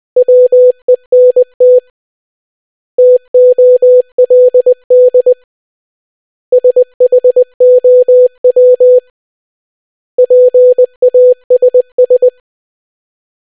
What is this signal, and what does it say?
Morse code, heard as a keyed tone, about 20 wpm, 500 Hz. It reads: WENT TOLD SHOW PASS